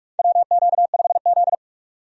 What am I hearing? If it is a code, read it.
Morse code, heard as a keyed tone, about 45 wpm, 700 Hz. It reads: WQ5Z
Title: USA Call Signs - 4 Characters - 45wpm